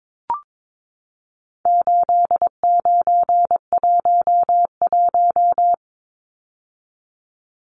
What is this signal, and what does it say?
Morse code, heard as a keyed tone, about 22 wpm, 700 Hz. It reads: 8911
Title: Numbers - 4-Digit 22wpm